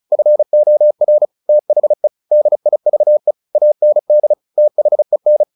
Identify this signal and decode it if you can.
Morse code, heard as a keyed tone, about 35 wpm, 600 Hz. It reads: FOR THE DIVE AND THEN
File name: the old man and the sea / the old man and the sea 1 35wpm